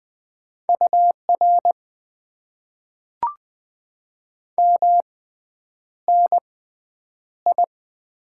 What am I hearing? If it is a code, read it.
Morse code, heard as a keyed tone, about 20 wpm, spaced as for 9 wpm, 700 Hz. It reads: UR MNI
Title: Top 90 QSO Elements - 6x Character Spacing - 20wpm